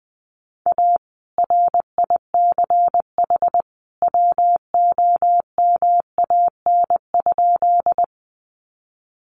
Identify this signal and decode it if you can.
Morse code, heard as a keyed tone, about 20 wpm, 700 Hz. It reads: A RICH WOMAN?
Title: Sentences from Top 1100 Words 20wpm